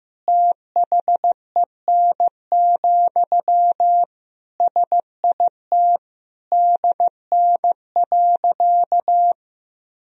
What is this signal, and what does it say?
Morse code, heard as a keyed tone, about 15 wpm, 700 Hz. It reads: THEN, SIT DN.
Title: Sentences from Top 500 Words and Abbreviations 15wpm